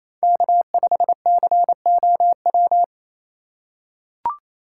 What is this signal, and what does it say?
Morse code, heard as a keyed tone, about 28 wpm, 700 Hz. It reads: K5COW
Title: USA Call Signs - 5 Characters - 4x Character Spacing - 28wpm